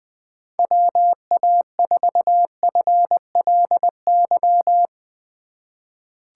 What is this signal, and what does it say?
Morse code, heard as a keyed tone, about 21 wpm, 700 Hz. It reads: WA4FLY
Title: USA Call Signs - 2x Character Spacing - 20wpm